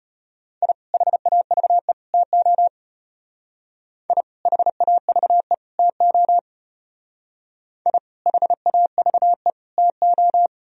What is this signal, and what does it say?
Morse code, heard as a keyed tone, about 32 wpm, 700 Hz. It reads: I HAVE TO I HAVE TO I HAVE TO